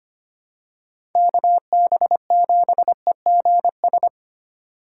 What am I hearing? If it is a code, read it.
Morse code, heard as a keyed tone, about 25 wpm, 700 Hz. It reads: KB7EGS
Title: USA Call Signs - 8x Character Spacing - 25wpm